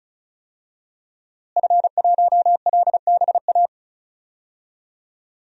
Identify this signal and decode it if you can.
Morse code, heard as a keyed tone, about 34 wpm, 700 Hz. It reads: F1LBA
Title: DXCC-100 Call Signs - 5 Characters - Repeated 2x - 35wpm